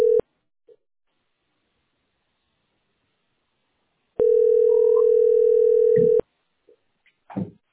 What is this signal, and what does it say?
Ringback tone (North America), 440 + 480 Hz.